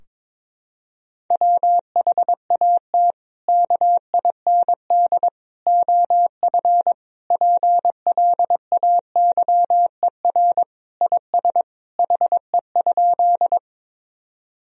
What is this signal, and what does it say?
Morse code, heard as a keyed tone, about 22 wpm, 700 Hz. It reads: WHAT KIND OF PLAYER IS HE?
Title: Sentences from Top 1000 Words - Review Part 4 - 22wpm